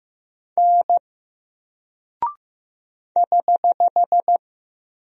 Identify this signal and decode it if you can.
Morse code, heard as a keyed tone, about 15 wpm, 700 Hz. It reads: N <HH>
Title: Single Character (Letter:Number:Punctuation:Prosign) 15wpm